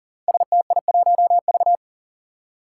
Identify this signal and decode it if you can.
Morse code, heard as a keyed tone, about 40 wpm, 700 Hz. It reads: STI1V
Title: Five Challenging Letter-Number Combinations 40wpm